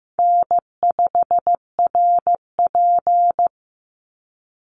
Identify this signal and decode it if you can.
Morse code, heard as a keyed tone, about 14 wpm, 700 Hz. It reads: N5RP
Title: Call Signs Repeated 3x - 15wpm